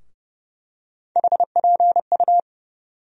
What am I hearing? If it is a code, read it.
Morse code, heard as a keyed tone, about 30 wpm, 700 Hz. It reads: HPU
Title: USA Call Sign Postfixes - 3 Letters 30wpm